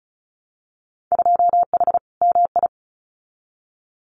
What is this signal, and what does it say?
Morse code, heard as a keyed tone, about 36 wpm, 700 Hz. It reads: 2H MI
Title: Winter Field Day Exchanges Encore 35wpm